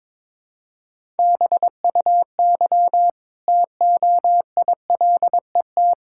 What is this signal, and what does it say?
Morse code, heard as a keyed tone, about 21 wpm, 700 Hz. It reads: BUY TOILET
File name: practice / 22wpm / Easy Sentences with Hints - 22wpm